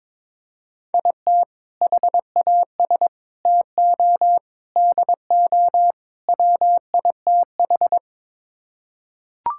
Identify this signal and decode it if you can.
Morse code, heard as a keyed tone, about 22 wpm, 700 Hz. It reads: IT HAS TO DO WITH
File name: practice / 22wpm / Build A Phrase - 5-Word Phrases - 22wpm